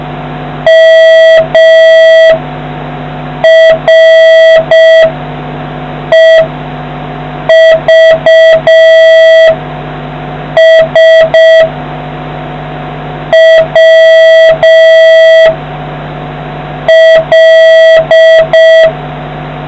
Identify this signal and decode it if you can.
Morse code, heard as a keyed tone, about 6 wpm, 650 Hz. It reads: M R E V S W L